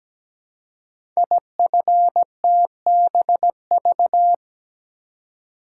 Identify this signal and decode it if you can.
Morse code, heard as a keyed tone, about 17 wpm, 700 Hz. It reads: IFTBV
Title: Code Groups - 5 Letters - 17wpm